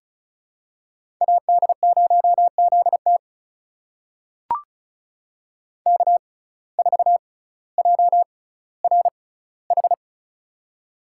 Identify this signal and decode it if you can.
Morse code, heard as a keyed tone, about 34 wpm, spaced as for 18 wpm, 700 Hz. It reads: AD0ZT K4JRH
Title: USA Call Signs - 5 Characters - 6x Character Spacing - 35wpm